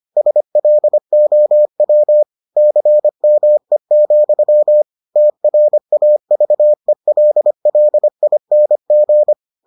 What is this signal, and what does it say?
Morse code, heard as a keyed tone, about 25 wpm, 600 Hz. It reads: SLOW CME, TRAVELLING